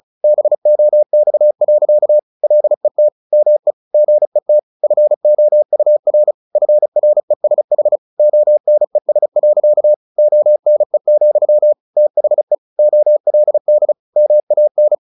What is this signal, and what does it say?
Morse code, heard as a keyed tone, about 34 wpm, 600 Hz. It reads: BOX. LET ME GET FOUR FRESH ONES. ONE, THE OLD MAN
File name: the old man and the sea / the old man and the sea 1 35wpm